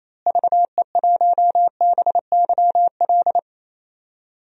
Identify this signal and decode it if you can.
Morse code, heard as a keyed tone, about 27 wpm, 700 Hz. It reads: VE1BYL
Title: DXCC-100 Call Signs - 6 Characters - Repeated 3x - 28wpm